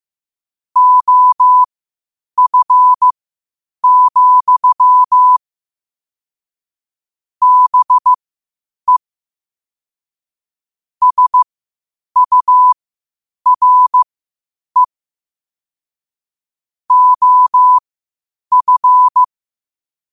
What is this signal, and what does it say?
Morse code, heard as a keyed tone, about 15 wpm, spaced as for 8 wpm, 1000 Hz. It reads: OF, BE SURE OF
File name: qotdfiles / 8wpm